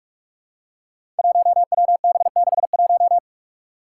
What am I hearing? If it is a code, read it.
Morse code, heard as a keyed tone, about 46 wpm, 700 Hz. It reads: 1WB61
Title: Five Challenging Letter-Number Combinations 45wpm